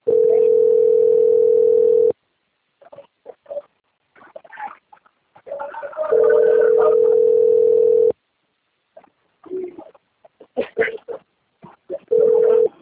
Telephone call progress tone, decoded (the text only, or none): ringback tone